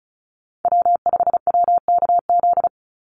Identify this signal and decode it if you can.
Morse code, heard as a keyed tone, about 34 wpm, 700 Hz. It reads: W5WKZ
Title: USA Call Signs 35wpm